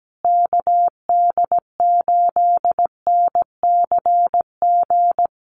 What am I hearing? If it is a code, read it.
Morse code, heard as a keyed tone, about 17 wpm, 700 Hz. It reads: KD8NCG
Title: USA Call Signs Repeated 3x 17wpm